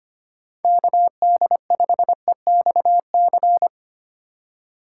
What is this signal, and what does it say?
Morse code, heard as a keyed tone, about 25 wpm, 700 Hz. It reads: KD5EXC
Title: USA Call Signs - 6 Characters - 4x Character Spacing - 25wpm